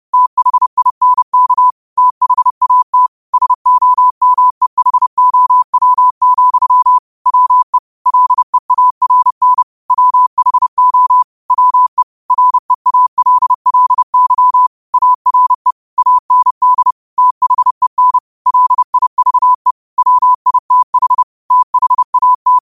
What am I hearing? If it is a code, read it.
Morse code, heard as a keyed tone, about 30 wpm, 1000 Hz. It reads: THINK THAT SOMEHOW, WE LEARN WHO WE REALLY ARE AND THEN LIVE WITH THAT